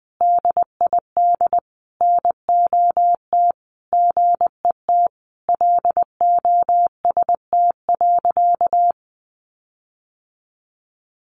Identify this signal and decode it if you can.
Morse code, heard as a keyed tone, about 20 wpm, 700 Hz. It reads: DID NOT GET LOST.
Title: Sentences from Top 1400 Words 20wpm